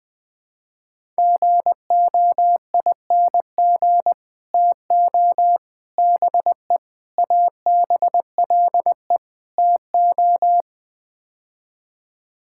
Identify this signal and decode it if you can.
Morse code, heard as a keyed tone, about 21 wpm, 690 Hz. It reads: GOING TO BE ABLE TO
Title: Build A Phrase - 5-Word Phrases - 20wpm